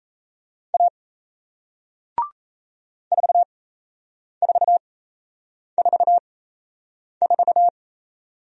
Morse code, heard as a keyed tone, about 37 wpm, spaced as for 8 wpm, 700 Hz. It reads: A 4444